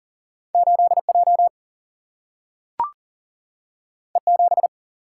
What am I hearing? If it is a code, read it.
Morse code, heard as a keyed tone, about 40 wpm, 700 Hz. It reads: 8 J E 7
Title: Single-Letter Single-Number 40wpm